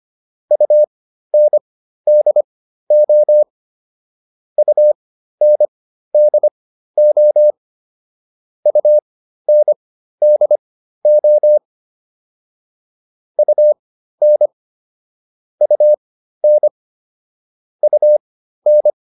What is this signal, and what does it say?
Morse code, heard as a keyed tone, about 25 wpm, spaced as for 13 wpm, 600 Hz. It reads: UNDO UNDO UNDO UN UN UN